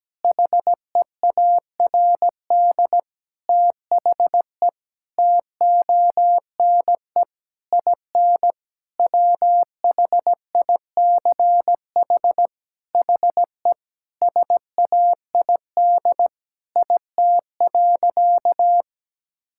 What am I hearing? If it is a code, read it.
Morse code, heard as a keyed tone, about 17 wpm, 700 Hz. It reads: HEARD THE TONE IN WHICH HE SAID IT.
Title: Sentences from Top 1600 Words 17wpm